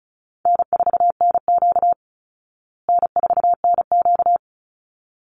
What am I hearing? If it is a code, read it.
Morse code, heard as a keyed tone, about 34 wpm, 700 Hz. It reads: N4NQ N4NQ
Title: Call Signs Repeated 3x - 35wpm